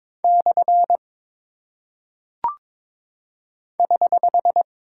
Morse code, heard as a keyed tone, about 22 wpm, 700 Hz. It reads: / <HH>
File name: practice / 22wpm / Single Punctuation-Prosign - 22wpm